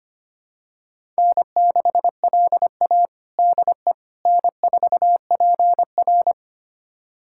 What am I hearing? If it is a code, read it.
Morse code, heard as a keyed tone, about 25 wpm, 700 Hz. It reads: N6LA DE N4PR
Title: USA Callsign DE Callsign - 4 Characters - 25wpm